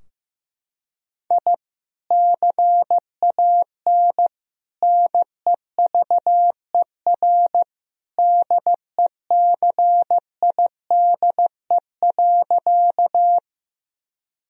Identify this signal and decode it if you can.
Morse code, heard as a keyed tone, about 16 wpm, 700 Hz. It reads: I CAN NEVER DECIDE.
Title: Sentences from Top 1000 Words - Review Part 3 - 15wpm